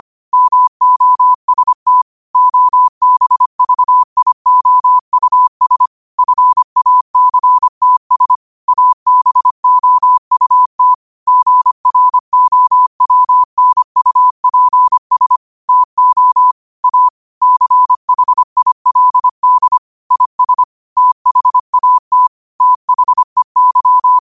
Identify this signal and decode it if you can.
Morse code, heard as a keyed tone, about 25 wpm, 1000 Hz. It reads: MOST OBVIOUS FACTS ABOUT GROWNUPS TO A CHILD IS THAT THEY
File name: qotdfiles / 25wpm